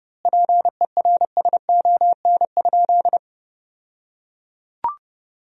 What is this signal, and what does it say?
Morse code, heard as a keyed tone, about 31 wpm, 700 Hz. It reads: PERSON?
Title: 2x Word-Spacing - Sentences from Top 400 Words - 30wpm